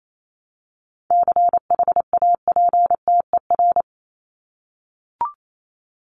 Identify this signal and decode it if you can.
Morse code, heard as a keyed tone, about 28 wpm, 700 Hz. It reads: CHAPTER
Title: Top 1751-2000 Words 28wpm